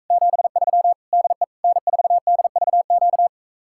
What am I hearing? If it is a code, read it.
Morse code, heard as a keyed tone, about 45 wpm, 700 Hz. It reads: 73 DE N4DVQ